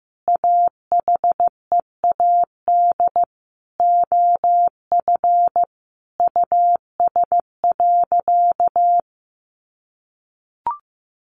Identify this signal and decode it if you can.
Morse code, heard as a keyed tone, about 15 wpm, 700 Hz. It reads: AHEAD OF US.
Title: Sentences from Top 1300 Words 15wpm